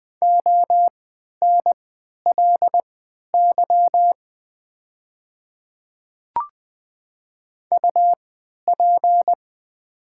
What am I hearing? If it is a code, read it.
Morse code, heard as a keyed tone, about 20 wpm, spaced as for 11 wpm, 700 Hz. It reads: ONLY UP
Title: 3x Character Spacing - Top 300 Words - 20wpm